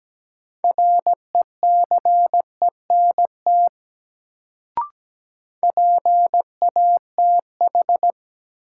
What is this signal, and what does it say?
Morse code, heard as a keyed tone, about 17 wpm, 700 Hz. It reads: RECENT PATH
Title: Sentences from Top 1400 Words - New Word Drill Alt - 17wpm